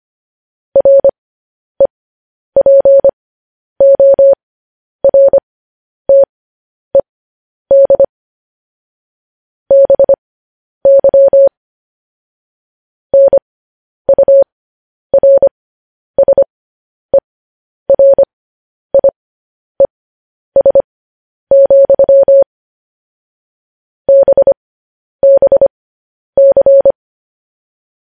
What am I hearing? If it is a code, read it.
Morse code, heard as a keyed tone, about 25 wpm, spaced as for 10 wpm, 550 Hz. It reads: REPORTED BY NURSERIES, BBC
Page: news headlines in morse code at 10 wpm